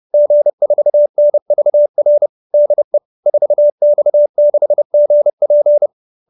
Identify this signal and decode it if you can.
Morse code, heard as a keyed tone, about 30 wpm, 590 Hz. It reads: G4NVR DE 4X6GP
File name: QSO315 30